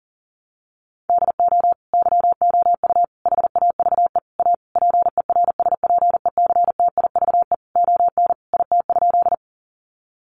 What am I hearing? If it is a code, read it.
Morse code, heard as a keyed tone, about 40 wpm, 700 Hz. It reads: DO YOU HAVE A PERSPECTIVE ON IT?
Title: Sentences from Top 1200 Words 40wpm